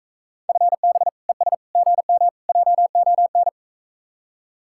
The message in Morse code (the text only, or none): FB ES GM JON